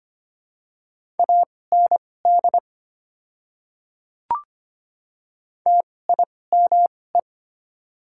Morse code, heard as a keyed tone, about 25 wpm, spaced as for 18 wpm, 700 Hz. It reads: AND TIME
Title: 2x Character Spacing - Top 100 Words - 25wpm